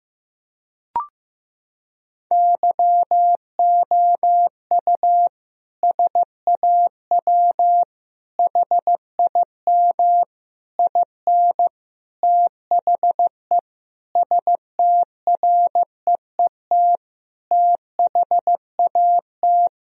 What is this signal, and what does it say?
Morse code, heard as a keyed tone, about 16 wpm, 700 Hz. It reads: YOU SAW HIM IN THE STREET THAT
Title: Sentences from Top 1000 Words - Review Part 3 - 15wpm